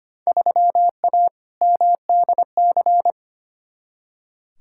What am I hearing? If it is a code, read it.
Morse code, heard as a keyed tone, about 25 wpm, 700 Hz. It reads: 3A MDC
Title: Field Day Exchanges - 25wpm